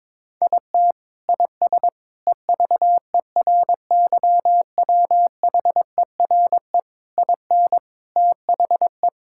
Morse code, heard as a keyed tone, about 22 wpm, 700 Hz. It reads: IT IS EVERYWHERE IN THE
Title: Sentences from Top 1600 Words 22wpm